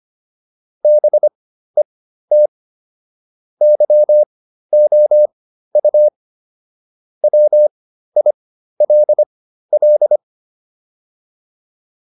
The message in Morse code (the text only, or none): BET YOU WILL